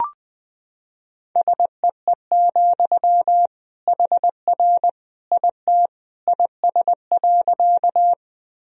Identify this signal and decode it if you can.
Morse code, heard as a keyed tone, about 21 wpm, 700 Hz. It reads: SEE, HR IT IS.
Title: Sentences from Top 200 Words and Abbreviations 20wpm